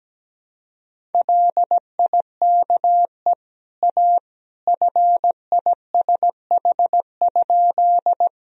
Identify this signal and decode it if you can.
Morse code, heard as a keyed tone, about 17 wpm, 700 Hz. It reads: LIKE A FISH?
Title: Sentences from Top 1000 Words - Review Part 1 - 17wpm